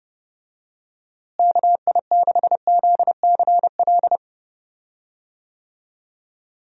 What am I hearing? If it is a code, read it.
Morse code, heard as a keyed tone, about 30 wpm, 700 Hz. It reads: KI6ZCL
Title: USA Call Signs - 6 Characters - Repeated 2x - 30wpm